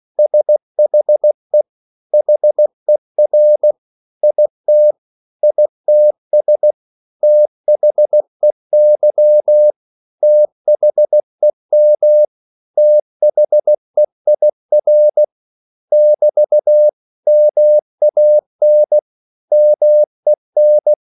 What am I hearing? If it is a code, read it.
Morse code, heard as a keyed tone, about 16 wpm, 600 Hz. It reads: SHE HER IT ITS THEY THEM THEIR <BT> MAN MEN
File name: CW / Practice / 500MostCommonEnglishWords16